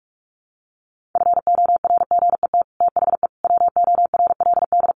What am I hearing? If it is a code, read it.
Morse code, heard as a keyed tone, about 44 wpm, 700 Hz. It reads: FORGET THE WORLD
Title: Sentences from Top 1200 Words 45wpm